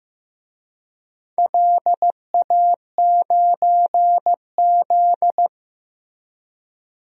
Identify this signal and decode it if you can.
Morse code, heard as a keyed tone, about 15 wpm, 700 Hz. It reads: LA9Z
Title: DXCC-100 Call Signs - 4 Characters - Repeated 2x - 15wpm